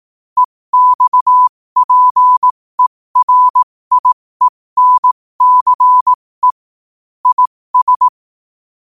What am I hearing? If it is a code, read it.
Morse code, heard as a keyed tone, about 18 wpm, spaced as for 15 wpm, 1000 Hz. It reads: EXPERIENCE IS
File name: qotdfiles / 15wpm